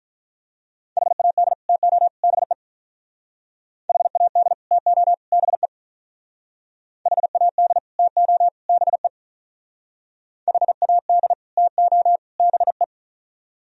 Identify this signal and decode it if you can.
Morse code, heard as a keyed tone, about 46 wpm, 700 Hz. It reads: HAD TO BE HAD TO BE HAD TO BE HAD TO BE